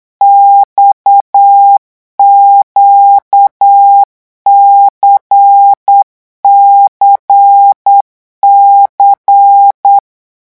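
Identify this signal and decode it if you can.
Morse code, heard as a keyed tone, about 8 wpm, 800 Hz. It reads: XQCCC
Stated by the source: Morse code revision lesson at 8 words per minute, from a French amateur radio course.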